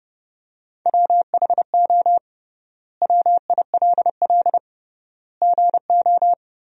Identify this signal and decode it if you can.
Morse code, heard as a keyed tone, about 30 wpm, 700 Hz. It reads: WHO WILL GO
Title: 3x Word-Spacing - Sentences from Top 100 Words - 30wpm